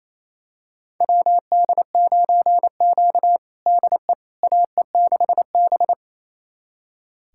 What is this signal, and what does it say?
Morse code, heard as a keyed tone, about 28 wpm, 700 Hz. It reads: WD9Q DE AE6B
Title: USA Callsign DE Callsign - 4 Characters - 28wpm